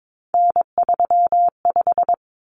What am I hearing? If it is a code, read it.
Morse code, heard as a keyed tone, about 22 wpm, 700 Hz. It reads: N35
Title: Common Cut Numbers - 3-Digit 22wpm